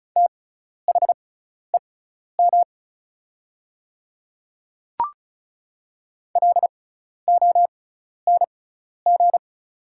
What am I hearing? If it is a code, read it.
Morse code, heard as a keyed tone, about 33 wpm, spaced as for 12 wpm, 690 Hz. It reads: THEM LONG
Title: 6x Character Spacing - Top 100 Words - 35wpm